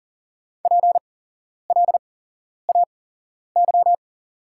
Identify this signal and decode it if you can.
Morse code, heard as a keyed tone, about 40 wpm, spaced as for 11 wpm, 690 Hz. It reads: PLAY
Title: 8x Character Spacing - Top 200 Words - 40wpm